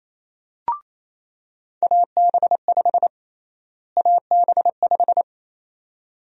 Morse code, heard as a keyed tone, about 28 wpm, 700 Hz. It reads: AB5 AB5
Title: USA Call Sign Prefixes - Repeated 2x - 28wpm